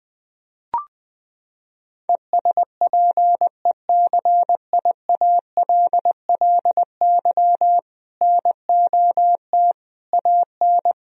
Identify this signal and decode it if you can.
Morse code, heard as a keyed tone, about 20 wpm, 700 Hz. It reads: ESPECIALLY NOT AN
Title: Sentences from Top 1000 Words - Review Part 4 - 20wpm